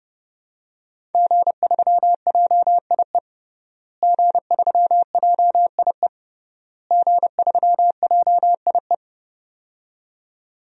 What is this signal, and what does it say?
Morse code, heard as a keyed tone, about 30 wpm, 700 Hz. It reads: G3JIE G3JIE G3JIE
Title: DXCC-100 Call Signs - 5 Characters - Repeated 3x - 30wpm